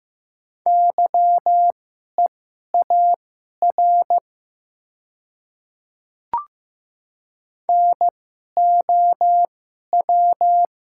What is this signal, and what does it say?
Morse code, heard as a keyed tone, about 15 wpm, spaced as for 11 wpm, 700 Hz. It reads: YEAR NOW
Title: 2x Character Spacing - Top 200 Words - 15wpm